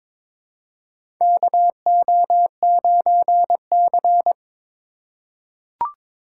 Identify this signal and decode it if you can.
Morse code, heard as a keyed tone, about 22 wpm, 700 Hz. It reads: KO9C
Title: USA Call Signs - 4 Characters - 8x Character Spacing - 22wpm